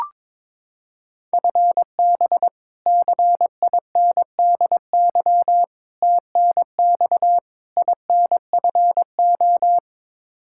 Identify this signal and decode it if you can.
Morse code, heard as a keyed tone, about 21 wpm, 700 Hz. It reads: FB CINDY TNX INFO
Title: Practice QSO - Exchange 3 - More Info - 22wpm